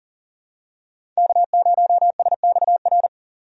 Morse code, heard as a keyed tone, about 40 wpm, 700 Hz. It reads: K0SXR
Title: USA Call Signs - 5 Characters - 40wpm